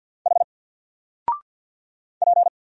Morse code, heard as a keyed tone, about 50 wpm, 700 Hz. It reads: H P
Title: Single Character (Letter:Number:Punctuation:Prosign) 50wpm